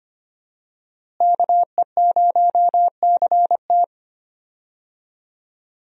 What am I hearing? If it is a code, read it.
Morse code, heard as a keyed tone, about 25 wpm, 700 Hz. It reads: KE0CT
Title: USA Call Signs - 5 Characters - Repeated 3x - 25wpm